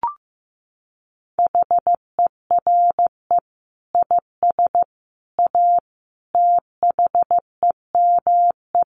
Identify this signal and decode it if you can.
Morse code, heard as a keyed tone, about 15 wpm, 700 Hz. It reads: HERE IS A THEME
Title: Sentences from Top 1100 Words 15wpm